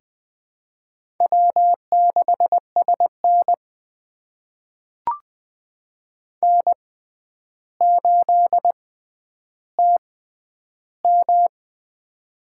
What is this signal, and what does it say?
Morse code, heard as a keyed tone, about 20 wpm, spaced as for 10 wpm, 700 Hz. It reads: W6SN N8TM
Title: USA Call Signs - 6x Character Spacing - 20wpm